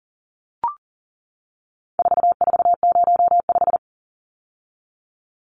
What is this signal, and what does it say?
Morse code, heard as a keyed tone, about 40 wpm, 700 Hz. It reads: 4405
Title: Numbers - 4-Digit 40wpm